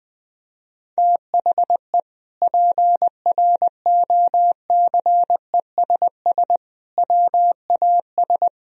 Morse code, heard as a keyed tone, about 21 wpm, 700 Hz. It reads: THE PROCESS WAS
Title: Sentences from Top 1000 Words - Review Part 3 - 20wpm